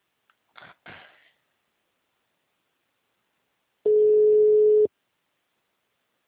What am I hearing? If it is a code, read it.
Ringback tone (Europe, 425 Hz).
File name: out- -2003- - - .14236